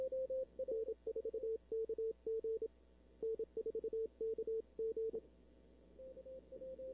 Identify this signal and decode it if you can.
Morse code, heard as a keyed tone, about 26 wpm, 440 Hz, with radio noise. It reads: N4KG N4KG
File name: arrl cw dx 2005